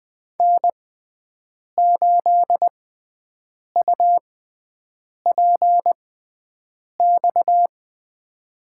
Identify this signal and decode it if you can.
Morse code, heard as a keyed tone, about 20 wpm, spaced as for 7 wpm, 700 Hz. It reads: N8UPX
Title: USA Call Signs - 6x Character Spacing - 20wpm